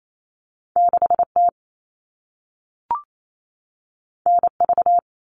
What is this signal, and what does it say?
Morse code, heard as a keyed tone, about 28 wpm, 700 Hz. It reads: 6T NV
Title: All Cut Numbers - 2-Digit 28wpm